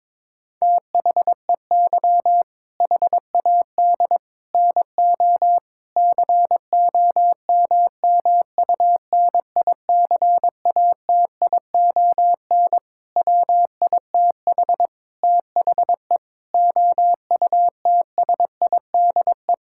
Morse code, heard as a keyed tone, about 21 wpm, 700 Hz. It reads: THEY HAD NO COMMUNICATION WITH THE OUTSIDE
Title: Sentences from Top 1000 Words - Review Part 6 - 22wpm